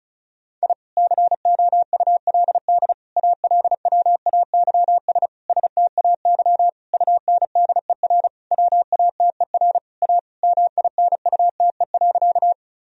Morse code, heard as a keyed tone, about 36 wpm, 700 Hz. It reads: I COULD ALWAYS STAY UNDER WATER A MINUTE.
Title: Sentences from Top 1000 Words - Review Part 1 - 35wpm